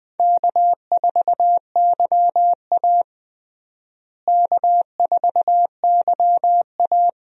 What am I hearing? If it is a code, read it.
Morse code, heard as a keyed tone, about 20 wpm, 700 Hz. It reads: K4YA K4YA
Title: USA Call Signs - 4 Characters - Repeated 2x - 20wpm